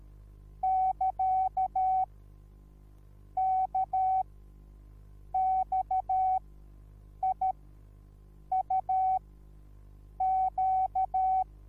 Morse code, heard as a keyed tone, about 13 wpm, spaced as for 6 wpm, 760 Hz. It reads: <KA>KXIUQ